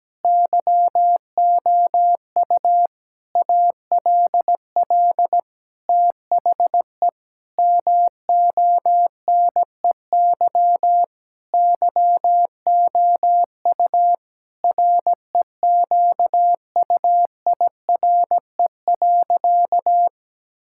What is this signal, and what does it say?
Morse code, heard as a keyed tone, about 17 wpm, 700 Hz. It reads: YOU ALL THE MONEY YOU REQUIRE.
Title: Sentences from Top 1000 Words - Review Part 6 - 17wpm